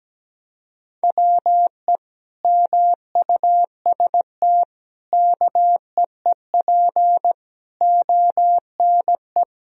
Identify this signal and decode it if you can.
Morse code, heard as a keyed tone, about 17 wpm, 700 Hz. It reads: WE MUST KEEP ONE
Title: Sentences from Top 1000 Words - Review Part 3 - 17wpm